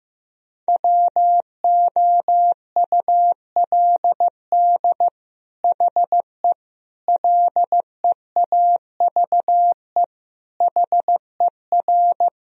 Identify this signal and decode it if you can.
Morse code, heard as a keyed tone, about 15 wpm, 700 Hz. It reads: WOULD HE LEAVE HER
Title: Sentences from Top 1000 Words - Review Part 1 - 15wpm